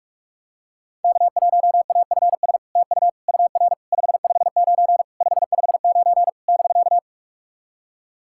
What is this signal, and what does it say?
Morse code, heard as a keyed tone, about 46 wpm, 700 Hz. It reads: K1AFS TU UR 559 559 <BK>
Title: SOTA QSO Practice 45wpm